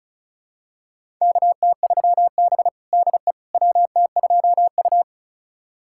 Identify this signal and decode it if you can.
Morse code, heard as a keyed tone, about 36 wpm, 700 Hz. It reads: KT3B DE WT2U
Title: USA Callsign DE Callsign - 4 Characters - 35wpm